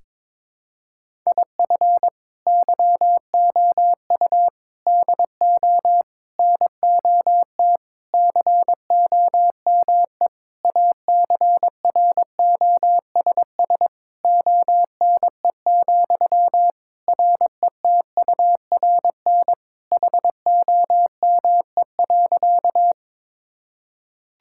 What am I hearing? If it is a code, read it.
Morse code, heard as a keyed tone, about 22 wpm, 700 Hz. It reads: IF YOU DO NOT COME ACROSS ONE, RETURN HOME.
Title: Sentences from Top 1000 Words - Review Part 5 - 22wpm